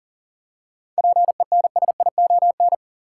Morse code, heard as a keyed tone, about 40 wpm, 700 Hz. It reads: PENSION
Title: Seven Letter Words - 40wpm